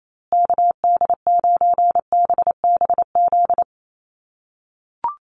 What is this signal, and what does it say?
Morse code, heard as a keyed tone, about 28 wpm, 700 Hz. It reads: KD9BBZ
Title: USA Call Signs Repeated 3x 28wpm